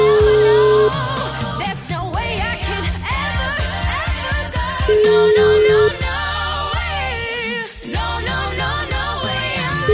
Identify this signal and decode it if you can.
Ringback tone (Europe, 425 Hz), with noise.